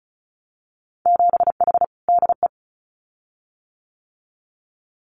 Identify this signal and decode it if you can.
Morse code, heard as a keyed tone, about 34 wpm, 700 Hz. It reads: 7H DE
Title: Winter Field Day Exchanges Encore 35wpm